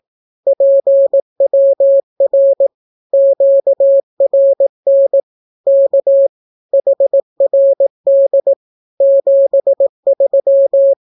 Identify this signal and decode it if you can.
Morse code, heard as a keyed tone, about 18 wpm, 550 Hz. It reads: PWR QRN K HRD 73
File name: naukapodstaw / skroty / skroty